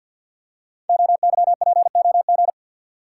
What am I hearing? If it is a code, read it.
Morse code, heard as a keyed tone, about 47 wpm, 700 Hz. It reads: OYPOG
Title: Code Groups - 5 Letters - 50wpm